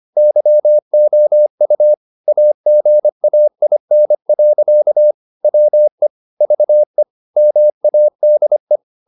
Morse code, heard as a keyed tone, about 25 wpm, 600 Hz. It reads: YOU AGAIN. WE VE MADE